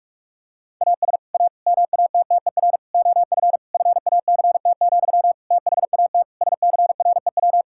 Morse code, heard as a keyed tone, about 46 wpm, 700 Hz. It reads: AS A MATTER OF FACT, THAT SCREW
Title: Sentences from Top 1600 Words 45wpm